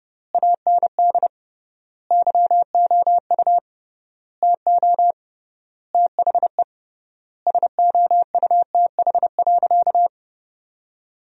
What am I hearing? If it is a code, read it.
Morse code, heard as a keyed tone, about 30 wpm, 700 Hz. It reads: AND YOU TO THE SOUTH.
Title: 3x Word-Spacing - Sentences from Top 400 Words - 30wpm